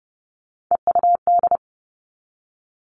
Morse code, heard as a keyed tone, about 30 wpm, 700 Hz. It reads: EUD
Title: Three Challenging Letter-Number Combinations 30wpm